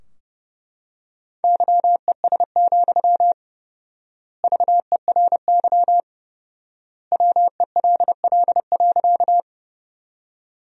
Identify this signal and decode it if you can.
Morse code, heard as a keyed tone, about 32 wpm, 700 Hz. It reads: YES, VERY WELL.
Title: 4x Word-Spacing - Sentences from Top 500 Words - 30wpm